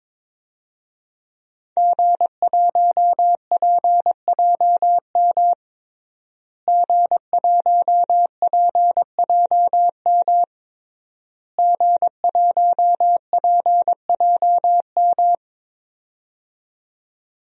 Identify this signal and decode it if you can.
Morse code, heard as a keyed tone, about 22 wpm, 700 Hz. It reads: G1PJM G1PJM G1PJM
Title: DXCC-100 Call Signs - 5 Characters - Repeated 3x - 22wpm